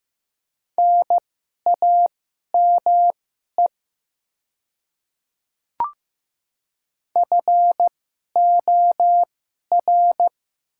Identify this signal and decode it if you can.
Morse code, heard as a keyed tone, about 15 wpm, spaced as for 11 wpm, 700 Hz. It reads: NAME FOR
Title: 2x Character Spacing - Top 200 Words - 15wpm